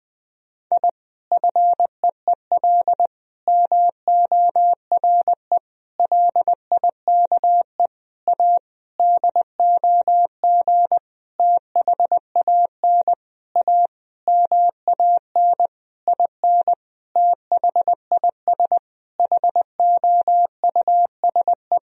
Morse code, heard as a keyed tone, about 21 wpm, 700 Hz. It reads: I FEEL MORE LIKE A DOG THAN A MAN IN THIS HOUSE
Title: Sentences from Top 1000 Words - Review Part 2 - 20wpm